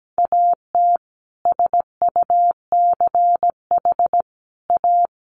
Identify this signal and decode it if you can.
Morse code, heard as a keyed tone, about 17 wpm, 700 Hz. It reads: AT SUCH A
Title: Sentences from Top 1000 Words 17wpm